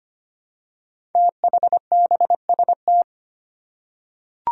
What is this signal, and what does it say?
Morse code, heard as a keyed tone, about 25 wpm, 700 Hz. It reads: THBST
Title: Code Groups - 5 Letters - 25wpm